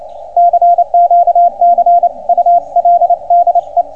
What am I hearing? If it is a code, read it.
Morse code, heard as a keyed tone, about 27 wpm, 690 Hz, with radio noise. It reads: CQ CUL DE